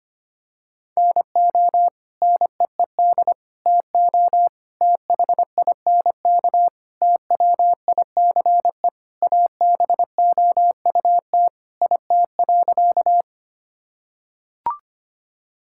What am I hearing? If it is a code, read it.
Morse code, heard as a keyed tone, about 25 wpm, 700 Hz. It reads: NO NEED TO THINK TWICE ABOUT IT.
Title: Sentences from Top 1600 Words 25wpm